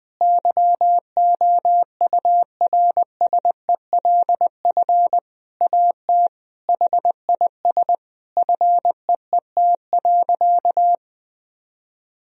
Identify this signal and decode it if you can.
Morse code, heard as a keyed tone, about 20 wpm, 700 Hz. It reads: YOURSELF AT HIS FEET.
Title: Sentences from Top 1000 Words - Review Part 1 - 20wpm